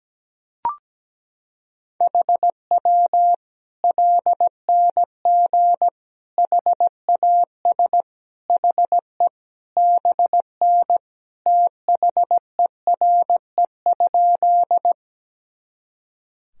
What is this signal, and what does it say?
Morse code, heard as a keyed tone, about 17 wpm, 700 Hz. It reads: HW LNG HAS HE BN THERE?
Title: Sentences from Top 100 Words and Abbreviations 17wpm